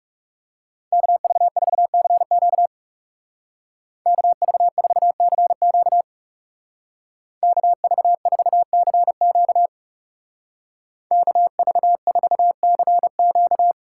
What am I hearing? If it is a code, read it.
Morse code, heard as a keyed tone, about 35 wpm, 700 Hz. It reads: KV4CQ KV4CQ KV4CQ KV4CQ
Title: Speed Racing - USA Call Signs 30wpm